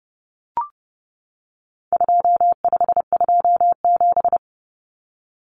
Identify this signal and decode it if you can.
Morse code, heard as a keyed tone, about 30 wpm, 700 Hz. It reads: 2527
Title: Numbers - 4-Digit 30wpm